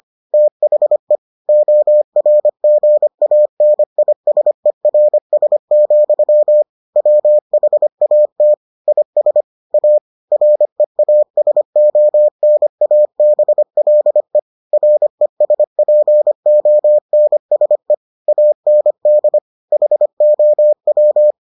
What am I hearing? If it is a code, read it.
Morse code, heard as a keyed tone, about 25 wpm, 600 Hz. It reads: THE ORGANISERS, WHAT IS A REASONABLE RESPONSE AND HOW